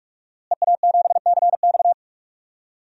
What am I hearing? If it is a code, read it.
Morse code, heard as a keyed tone, about 46 wpm, 700 Hz. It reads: EA7CX
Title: DXCC-100 Call Signs - 5 Characters - Repeated 2x - 45wpm